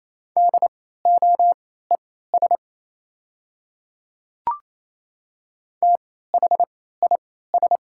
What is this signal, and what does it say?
Morse code, heard as a keyed tone, about 28 wpm, spaced as for 16 wpm, 690 Hz. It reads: DOES THIS
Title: 3x Character Spacing - Top 500 Words - 28wpm